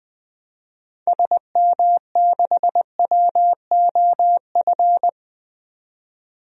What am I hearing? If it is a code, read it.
Morse code, heard as a keyed tone, about 20 wpm, 700 Hz. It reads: SM6WOF
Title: DXCC-100 Call Signs - 6 Characters - 20wpm